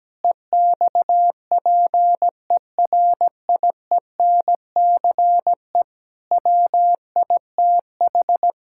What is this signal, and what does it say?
Morse code, heard as a keyed tone, about 17 wpm, 700 Hz. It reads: EXPERIENCE WITH
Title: Sentences from Top 1000 Words - Review Part 5 - 17wpm